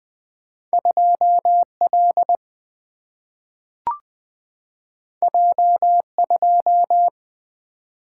Morse code, heard as a keyed tone, about 20 wpm, 700 Hz. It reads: 2 L J 2
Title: Single-Letter Single-Number 20wpm